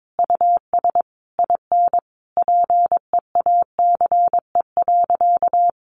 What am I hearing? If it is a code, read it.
Morse code, heard as a keyed tone, about 22 wpm, 700 Hz. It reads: US IN PEACE.
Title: Sentences from Top 1500 Words 22wpm